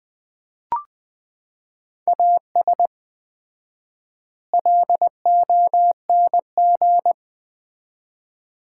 Morse code, heard as a keyed tone, about 20 wpm, 700 Hz. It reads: AS LONG
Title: 4x Word-Spacing - Sets of 4 Words - 20wpm